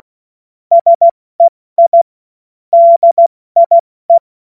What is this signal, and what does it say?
Morse code, heard as a keyed tone, about 16 wpm, 680 Hz. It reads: SEI DIE
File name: Morsekurs / Witze / Witz 30. August 2022 70 F80